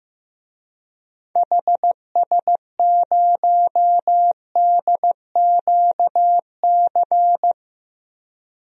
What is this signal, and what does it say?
Morse code, heard as a keyed tone, about 15 wpm, 700 Hz. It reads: HS0DQC
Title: DXCC-100 Call Signs - 6 Characters - Repeated 2x - 15wpm